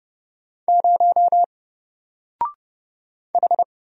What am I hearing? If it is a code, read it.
Morse code, heard as a keyed tone, about 30 wpm, 700 Hz. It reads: 0 H
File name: Single-Letter-Number-30wpm